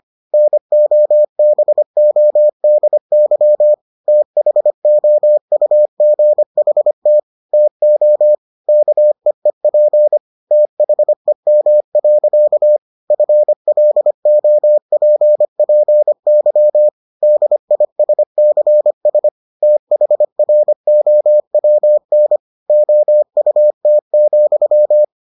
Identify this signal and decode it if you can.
Morse code, heard as a keyed tone, about 25 wpm, 600 Hz. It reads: NOBODY THOUGHT TO KEEP THEM. FLOPPY DISCS THROWN OUT,